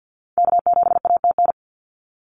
Morse code, heard as a keyed tone, about 50 wpm, 700 Hz. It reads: K7ATN
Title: NA SOTA Activator Callsigns 50wpm